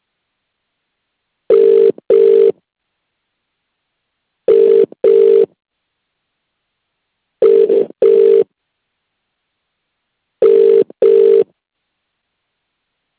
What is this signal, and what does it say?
Ringback tone (UK), 400 + 450 Hz.